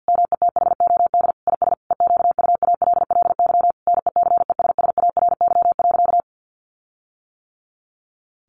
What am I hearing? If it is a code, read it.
Morse code, heard as a keyed tone, about 50 wpm, 700 Hz. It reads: METHOD IS EQUALLY NECESSARY.
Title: Sentences from Top 1400 Words 50wpm